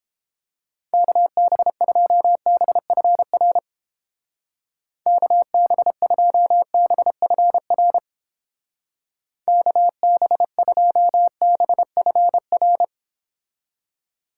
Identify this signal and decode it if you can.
Morse code, heard as a keyed tone, about 31 wpm, 700 Hz. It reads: KB2BFR KB2BFR KB2BFR